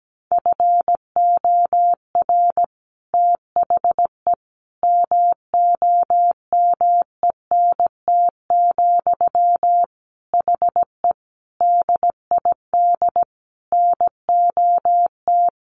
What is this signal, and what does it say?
Morse code, heard as a keyed tone, about 17 wpm, 700 Hz. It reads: FOR THE MOMENT, HE DID NOT
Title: Sentences from Top 1400 Words 17wpm